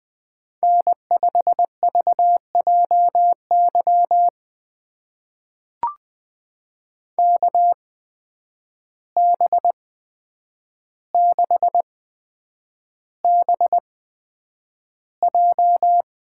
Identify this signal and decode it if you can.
Morse code, heard as a keyed tone, about 20 wpm, spaced as for 9 wpm, 700 Hz. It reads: N5VJY KB6BJ
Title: USA Call Signs - 5 Characters - 8x Character Spacing - 20wpm